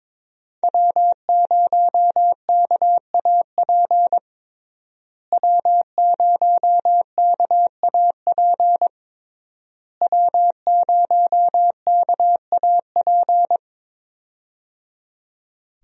Morse code, heard as a keyed tone, about 22 wpm, 700 Hz. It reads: W0KAP W0KAP W0KAP
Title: USA Call Signs - 5 Characters - Repeated 3x - 22wpm